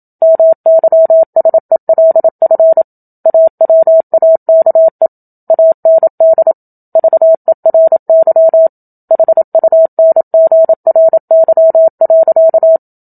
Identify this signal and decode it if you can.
Morse code, heard as a keyed tone, about 27 wpm, 650 Hz. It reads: MYSELF AWAKE AND VERY HUNGRY.
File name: Book Two / 01-Chapter 1 - 27 WPM